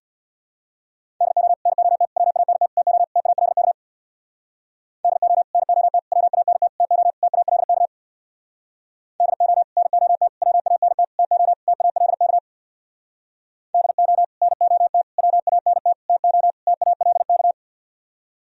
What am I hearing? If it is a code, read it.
Morse code, heard as a keyed tone, about 64 wpm, 700 Hz. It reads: DO NOT WANT TO TALK DO NOT WANT TO TALK DO NOT WANT TO TALK DO NOT WANT TO TALK